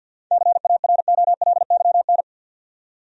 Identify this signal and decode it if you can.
Morse code, heard as a keyed tone, about 53 wpm, 700 Hz. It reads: CAROLYN
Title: USA Common Names - 50wpm